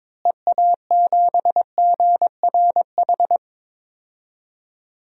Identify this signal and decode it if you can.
Morse code, heard as a keyed tone, about 21 wpm, 700 Hz. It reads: EA7GRH
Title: DXCC-100 Call Signs - 6 Characters - Repeated 2x - 22wpm